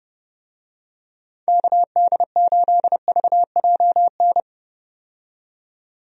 Morse code, heard as a keyed tone, about 31 wpm, 700 Hz. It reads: KD8VJN
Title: USA Call Signs - 6 Characters - 30wpm